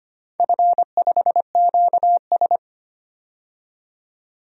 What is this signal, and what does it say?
Morse code, heard as a keyed tone, about 25 wpm, 700 Hz. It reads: F5QS
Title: DXCC-100 Call Signs - 4 Characters - 25wpm